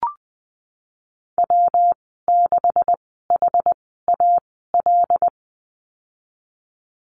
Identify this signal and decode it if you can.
Morse code, heard as a keyed tone, about 20 wpm, spaced as for 14 wpm, 700 Hz. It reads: W6HAL
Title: 128 Lesson 21g - Number 6 - Call Signs 20wpm